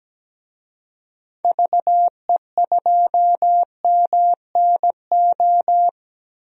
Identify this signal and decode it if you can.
Morse code, heard as a keyed tone, about 17 wpm, 700 Hz. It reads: VE2MNO
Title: DXCC-100 Call Signs - 6 Characters - 17wpm